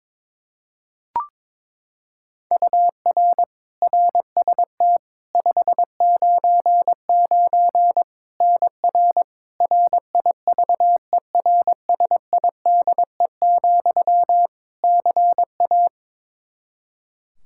Morse code, heard as a keyed tone, about 22 wpm, 700 Hz. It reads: UR RST 599 NR RIVERSIDE, CA
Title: Practice QSO - Exchange 1 - Beginning - 22wpm